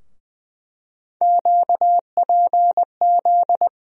Morse code, heard as a keyed tone, about 20 wpm, 700 Hz. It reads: QPZ
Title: USA Call Sign Postfixes - 3 Letters 20wpm